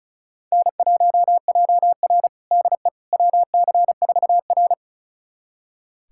Morse code, heard as a keyed tone, about 34 wpm, 700 Hz. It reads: N1JR DE WC4R
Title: USA Callsign DE Callsign - 4 Characters - 35wpm